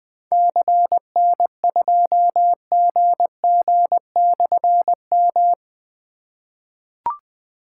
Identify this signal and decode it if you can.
Morse code, heard as a keyed tone, about 20 wpm, 700 Hz. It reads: CN2GG/M
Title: Difficult Call Signs 20wpm